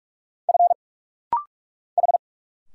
Morse code, heard as a keyed tone, about 46 wpm, 700 Hz. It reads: F H
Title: Single Letters - Rapid-Fire 45wpm